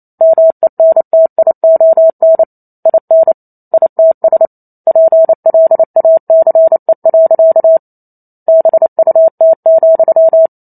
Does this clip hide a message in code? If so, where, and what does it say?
Yes: Morse code from 0.2 to 10.5 s, heard as a keyed tone, about 28 wpm, 650 Hz. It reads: MENTION IN ITS PLACE. BUT,